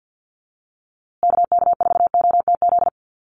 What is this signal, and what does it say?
Morse code, heard as a keyed tone, about 50 wpm, 700 Hz. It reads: KK4OTZ
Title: USA Call Signs Repeated 3x 50wpm